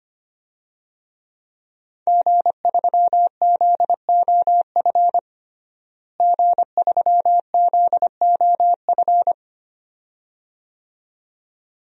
Morse code, heard as a keyed tone, about 25 wpm, 700 Hz. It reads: G3ZOF G3ZOF
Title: DXCC-100 Call Signs - 5 Characters - Repeated 2x - 25wpm